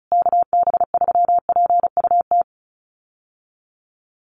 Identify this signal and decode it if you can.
Morse code, heard as a keyed tone, about 34 wpm, 700 Hz. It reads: KB3PUT
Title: USA Call Signs Repeated 3x 35wpm